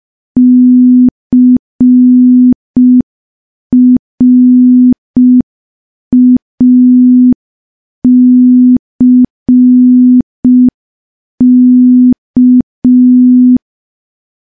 Morse code, heard as a keyed tone, about 5 wpm, 250 Hz. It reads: CRACK
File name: morse / morse-training-05WPM